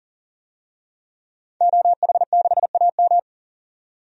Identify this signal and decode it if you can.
Morse code, heard as a keyed tone, about 40 wpm, 700 Hz. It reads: OH6AM
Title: DXCC-100 Call Signs - 5 Characters - 40wpm